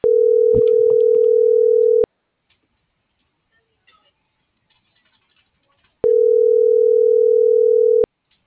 Ringback tone (North America), 440 + 480 Hz.